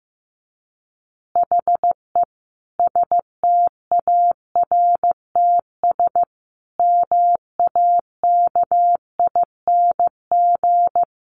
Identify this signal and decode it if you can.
Morse code, heard as a keyed tone, about 15 wpm, 700 Hz. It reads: HE STARTS MAKING
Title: Additional Easy Phrases 15wpm